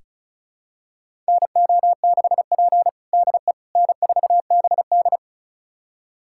Morse code, heard as a keyed tone, about 36 wpm, 700 Hz. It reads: NO6P DE N4BD
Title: USA Callsign DE Callsign - 4 Characters - 35wpm